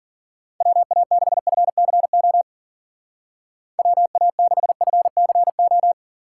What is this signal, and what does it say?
Morse code, heard as a keyed tone, about 49 wpm, 700 Hz. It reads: WA6FCO WA6FCO
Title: Speed Racing - USA 2x3 Call Signs 40wpm